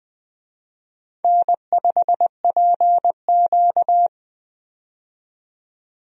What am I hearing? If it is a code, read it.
Morse code, heard as a keyed tone, about 20 wpm, 700 Hz. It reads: N5PQ
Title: USA Call Signs - 4 Characters - 20wpm